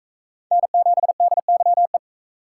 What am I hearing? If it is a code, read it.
Morse code, heard as a keyed tone, about 42 wpm, 700 Hz. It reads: N7DYE